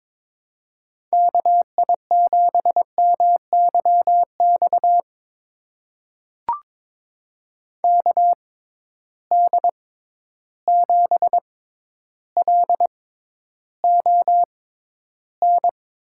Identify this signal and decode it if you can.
Morse code, heard as a keyed tone, about 22 wpm, spaced as for 11 wpm, 700 Hz. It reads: KI7MYX KD7LON
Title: USA Call Signs - 6 Characters - 6x Character Spacing - 22wpm